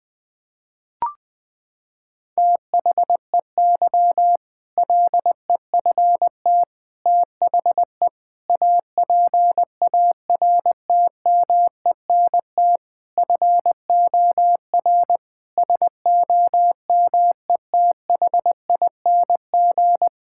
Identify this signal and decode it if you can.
Morse code, heard as a keyed tone, about 20 wpm, 700 Hz. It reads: THEY LEFT THE APARTMENT FOR SOMETHING
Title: Sentences from Top 1000 Words - Review Part 6 - 20wpm